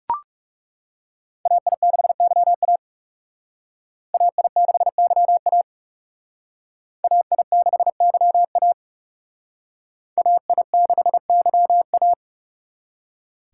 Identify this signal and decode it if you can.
Morse code, heard as a keyed tone, about 39 wpm, 700 Hz. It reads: AI6YA AI6YA AI6YA AI6YA